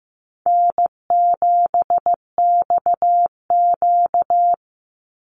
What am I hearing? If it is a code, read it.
Morse code, heard as a keyed tone, about 15 wpm, 700 Hz. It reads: N7XQ
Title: USA Call Signs 15wpm